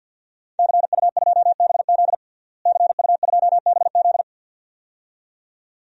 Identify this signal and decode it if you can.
Morse code, heard as a keyed tone, about 52 wpm, 700 Hz. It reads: CU2BZ CU2BZ
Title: DXCC-100 Call Signs - 5 Characters - Repeated 2x - 50wpm